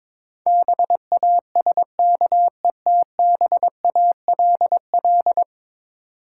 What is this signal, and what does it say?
Morse code, heard as a keyed tone, about 22 wpm, 700 Hz. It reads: BASKETBALL
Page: morse-code-ninja-practice-22wpm